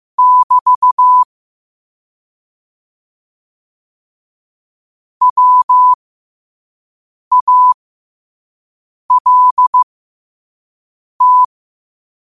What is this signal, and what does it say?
Morse code, heard as a keyed tone, about 15 wpm, spaced as for 5 wpm, 1000 Hz. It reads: <BT> WALT
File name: qotdfiles / 5wpm